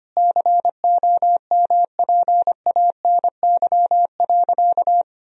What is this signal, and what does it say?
Morse code, heard as a keyed tone, about 25 wpm, 700 Hz. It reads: COMPANY.
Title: Sentences from Top 1000 Words - Review Part 3 - 25wpm